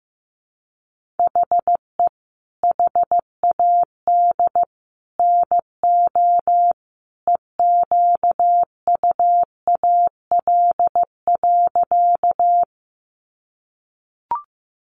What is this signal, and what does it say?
Morse code, heard as a keyed tone, about 15 wpm, 700 Hz. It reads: HE HAD NO EQUAL.
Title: Sentences from Top 1400 Words 15wpm